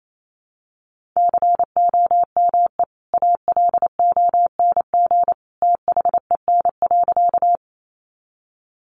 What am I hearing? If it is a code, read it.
Morse code, heard as a keyed tone, about 28 wpm, 700 Hz. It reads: COME ALONG THEN.
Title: Sentences from Top 700 Words 28wpm